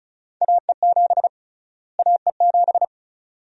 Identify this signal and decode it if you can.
Morse code, heard as a keyed tone, about 34 wpm, 700 Hz. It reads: AE7 AE7
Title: USA Call Sign Prefixes - Repeated 2x - 35wpm